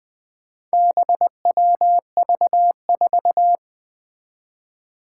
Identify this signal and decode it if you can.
Morse code, heard as a keyed tone, about 21 wpm, 700 Hz. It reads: BWV4
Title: Four Challenging Letter-Number Combinations 20wpm